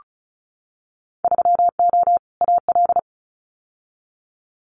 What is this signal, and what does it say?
Morse code, heard as a keyed tone, about 36 wpm, 690 Hz. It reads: 3O AL
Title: Winter Field Day Exchanges 35wpm